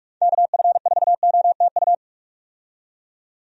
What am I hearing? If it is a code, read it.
Morse code, heard as a keyed tone, about 45 wpm, 700 Hz. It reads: KF4OTU
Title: USA Call Signs - 6 Characters - 45wpm